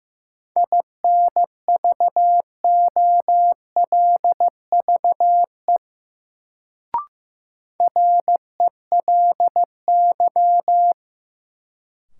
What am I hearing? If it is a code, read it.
Morse code, heard as a keyed tone, about 15 wpm, 700 Hz. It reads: INVOLVE RELY
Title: Sentences from Top 1200 Words - New Word Drill - 15wpm